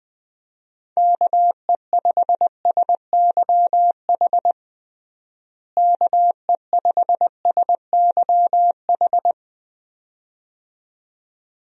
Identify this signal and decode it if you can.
Morse code, heard as a keyed tone, about 21 wpm, 700 Hz. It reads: KE5SYH KE5SYH
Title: USA Call Signs - 6 Characters - Repeated 2x - 20wpm